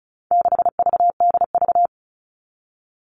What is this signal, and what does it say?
Morse code, heard as a keyed tone, about 34 wpm, 700 Hz. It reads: 6VDV
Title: All Cut Numbers - 4-Digit 35wpm